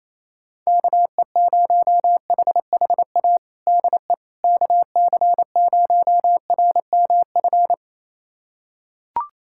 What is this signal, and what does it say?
Morse code, heard as a keyed tone, about 28 wpm, 700 Hz. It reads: KE0HHA DE KC0RMF
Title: USA Callsign DE Callsign 28wpm